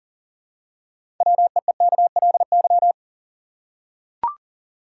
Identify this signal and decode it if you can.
Morse code, heard as a keyed tone, about 40 wpm, 700 Hz. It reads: WEEKLY
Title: Six Letter Words - 40wpm